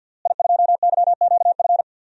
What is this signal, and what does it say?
Morse code, heard as a keyed tone, about 45 wpm, 700 Hz. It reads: I2YQF
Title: DXCC-100 Call Signs - 5 Characters - Repeated 3x - 50wpm